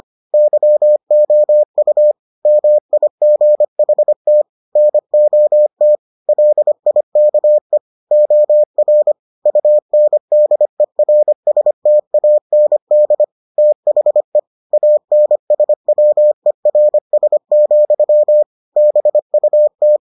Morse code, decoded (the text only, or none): YOU MIGHT NOT LIKE OR UNDERSTAND THE ANSWERS, BUT